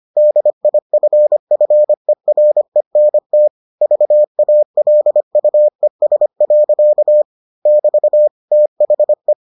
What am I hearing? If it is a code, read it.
Morse code, heard as a keyed tone, about 25 wpm, 600 Hz. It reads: DIFFERENT VALUES. <BT> THE